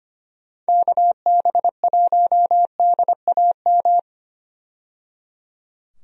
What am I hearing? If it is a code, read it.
Morse code, heard as a keyed tone, about 25 wpm, 700 Hz. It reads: KB1DAM
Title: USA Call Signs - 6 Characters - 25wpm